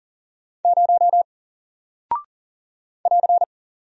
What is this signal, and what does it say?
Morse code, heard as a keyed tone, about 39 wpm, 700 Hz. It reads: 0 <AR>
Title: Single Character (Letter:Number:Punctuation:Prosign) Alt 40wpm